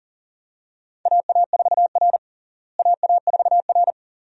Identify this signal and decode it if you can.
Morse code, heard as a keyed tone, about 40 wpm, 700 Hz. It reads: AA4R AA4R
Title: USA Call Signs - 4 Characters - Repeated 2x - 40wpm